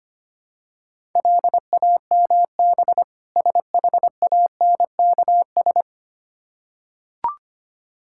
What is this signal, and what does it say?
Morse code, heard as a keyed tone, about 25 wpm, 700 Hz. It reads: LAMB SHANKS
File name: practice / 25wpm / Difficult Words with Hints - 25wpm